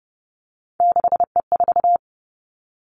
Morse code, heard as a keyed tone, about 30 wpm, 700 Hz. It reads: 6E4
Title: Three Challenging Letter-Number Combinations 30wpm